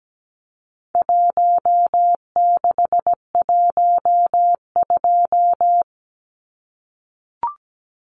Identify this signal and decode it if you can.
Morse code, heard as a keyed tone, about 17 wpm, 700 Hz. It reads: 1612
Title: Numbers - Contest Style - 4-Digit 17wpm